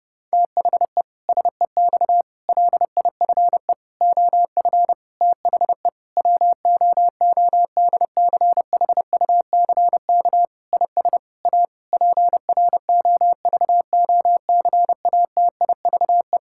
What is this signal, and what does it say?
Morse code, heard as a keyed tone, about 30 wpm, 700 Hz. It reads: THE SEX LIFE OF THE WOODCHUCK IS A PROVOCATIVE
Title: Pangram Challenge v2 - 30wpm